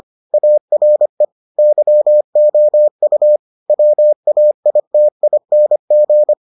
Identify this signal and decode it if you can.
Morse code, heard as a keyed tone, about 25 wpm, 600 Hz. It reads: ARE YOU WAITING